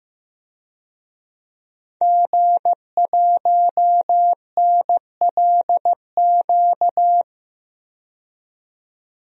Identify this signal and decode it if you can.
Morse code, heard as a keyed tone, about 15 wpm, 700 Hz. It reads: G1NLQ
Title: DXCC-100 Call Signs - 5 Characters - 15wpm